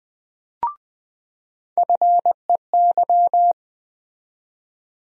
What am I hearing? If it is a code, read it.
Morse code, heard as a keyed tone, about 20 wpm, 700 Hz. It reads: FEY
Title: USA Call Sign Postfixes - 3 Letters 20wpm